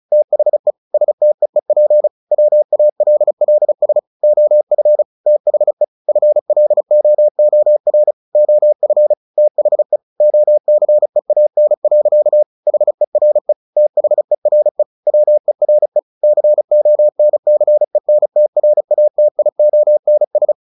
Morse code, heard as a keyed tone, about 35 wpm, 600 Hz. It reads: THE STEEP WALLS OF THE FLOOR OF THE OCEAN. HERE THERE WERE CONCENTRATIONS
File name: the old man and the sea / the old man and the sea 1 35wpm